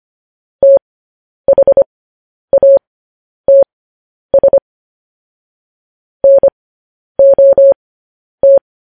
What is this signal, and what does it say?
Morse code, heard as a keyed tone, about 25 wpm, spaced as for 10 wpm, 550 Hz. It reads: THATS NOT
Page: news-headlines-in-morse-code-at-10-wpm